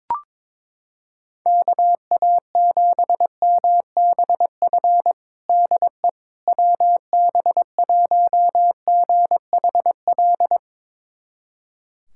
Morse code, heard as a keyed tone, about 22 wpm, 700 Hz. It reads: KA7MBF DE WB1GHL
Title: USA Callsign DE Callsign - 6 Characters - 22wpm